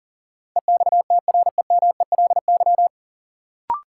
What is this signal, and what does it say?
Morse code, heard as a keyed tone, about 39 wpm, 700 Hz. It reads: EXTREMELY
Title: Sentences from Top 1200 Words - New Word Drill Alt - 40wpm